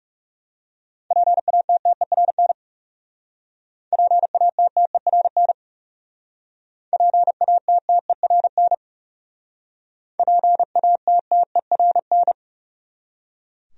Morse code, heard as a keyed tone, about 38 wpm, 700 Hz. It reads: PATTERN PATTERN PATTERN PATTERN